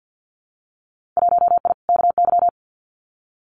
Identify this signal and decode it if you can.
Morse code, heard as a keyed tone, about 50 wpm, 700 Hz. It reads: 1I KY
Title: Winter Field Day Exchanges 50wpm